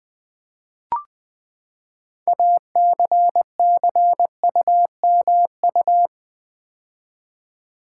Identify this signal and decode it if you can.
Morse code, heard as a keyed tone, about 20 wpm, 700 Hz. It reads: ACCUMU
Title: Build A Word - 6 Letters At A Time - 12-Letter Words - 20wpm